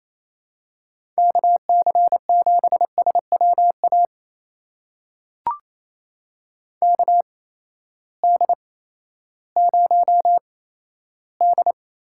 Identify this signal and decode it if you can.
Morse code, heard as a keyed tone, about 28 wpm, spaced as for 14 wpm, 700 Hz. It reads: KC7SWA KD0D
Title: USA Call Signs - 6 Characters - 8x Character Spacing - 28wpm